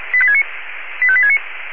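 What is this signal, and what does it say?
Five-tone selective call (ZVEI), heard as quick runs of five tones, tones of about 70 ms. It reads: 85869
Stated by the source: Received by SDR as a ZVEI five-tone selective call.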